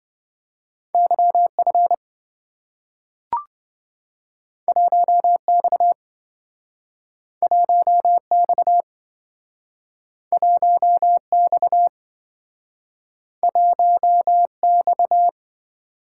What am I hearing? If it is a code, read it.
Morse code, heard as a keyed tone, about 27 wpm, 700 Hz. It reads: YF 1X 1X 1X 1X